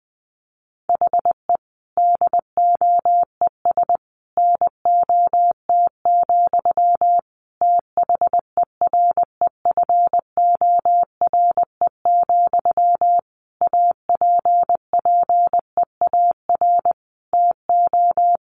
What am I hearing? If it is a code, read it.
Morse code, heard as a keyed tone, about 20 wpm, 700 Hz. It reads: HE DOES NOT, THEREFORE, APPEAR TO
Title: Sentences from Top 1200 Words 20wpm